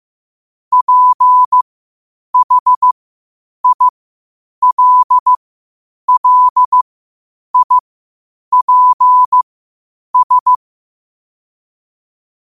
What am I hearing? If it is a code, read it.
Morse code, heard as a keyed tone, about 15 wpm, spaced as for 8 wpm, 1000 Hz. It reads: PHILLIPS
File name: qotdfiles / 8wpm